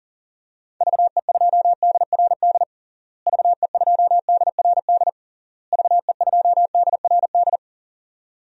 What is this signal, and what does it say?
Morse code, heard as a keyed tone, about 40 wpm, 700 Hz. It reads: VE2DRD VE2DRD VE2DRD
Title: DXCC-100 Call Signs - 6 Characters - Repeated 3x - 40wpm